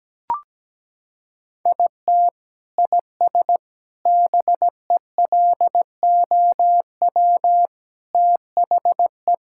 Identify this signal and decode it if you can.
Morse code, heard as a keyed tone, about 17 wpm, 700 Hz. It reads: IT IS BELOW THE
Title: Sentences from Top 1000 Words - Review Part 5 - 17wpm